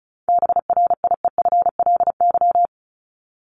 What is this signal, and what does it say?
Morse code, heard as a keyed tone, about 34 wpm, 700 Hz. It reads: BRIEFLY
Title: LY Suffix - Top 100 Words with Base Words 35wpm